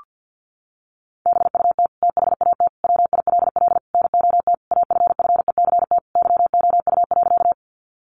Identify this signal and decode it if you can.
Morse code, heard as a keyed tone, about 50 wpm, 700 Hz. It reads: BUT THAT WILL NOT AFFECT YOU.
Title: Sentences from Top 900 Words 50wpm